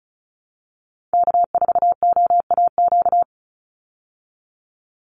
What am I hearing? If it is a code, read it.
Morse code, heard as a keyed tone, about 34 wpm, 700 Hz. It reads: K4OAQ
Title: Call Signs Repeated 3x - 35wpm